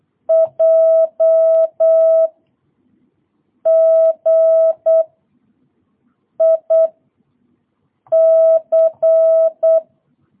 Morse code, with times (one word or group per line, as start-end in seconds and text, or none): 0.3-2.3 J
3.7-5.0 G
6.4-6.9 I
8.1-9.8 C